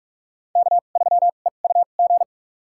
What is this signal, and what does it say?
Morse code, heard as a keyed tone, about 42 wpm, spaced as for 31 wpm, 700 Hz. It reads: K3EUG
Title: USA Call Signs - 5 Characters - 2x Character Spacing - 45wpm